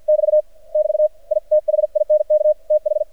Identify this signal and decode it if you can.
Morse code, heard as a keyed tone, about 54 wpm, 610 Hz. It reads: <BT> <BT> I THINK TH
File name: archived / CRSnet